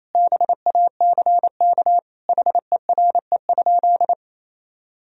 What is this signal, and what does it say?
Morse code, heard as a keyed tone, about 28 wpm, 700 Hz. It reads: BACK HERE?
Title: Sentences from Top 1000 Words - Review Part 4 - 28wpm